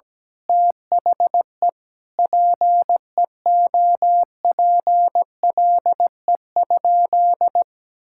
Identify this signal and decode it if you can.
Morse code, heard as a keyed tone, about 17 wpm, 700 Hz. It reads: THE PEOPLE?
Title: Sentences from Top 1000 Words - Review Part 5 - 17wpm